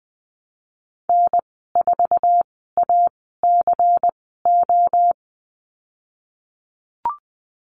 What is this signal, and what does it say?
Morse code, heard as a keyed tone, about 20 wpm, spaced as for 15 wpm, 700 Hz. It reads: N4ACO
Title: 161 Lesson 26g - Letter g - Call Signs 20wpm